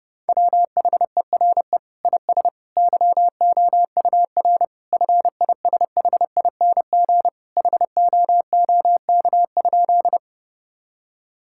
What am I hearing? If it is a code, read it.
Morse code, heard as a keyed tone, about 30 wpm, 700 Hz. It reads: WHERE IS YOUR FISHING HOOK?
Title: Sentences from Top 1600 Words 30wpm